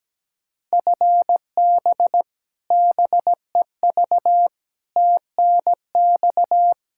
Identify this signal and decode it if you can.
Morse code, heard as a keyed tone, about 17 wpm, 700 Hz. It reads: FB BEV TNX
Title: Practice QSO - Exchange 3 - More Info - 17wpm